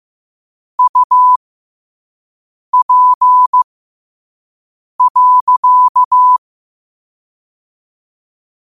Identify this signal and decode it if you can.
Morse code, heard as a keyed tone, about 15 wpm, spaced as for 5 wpm, 1000 Hz. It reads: UP.
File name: qotdfiles / 5wpm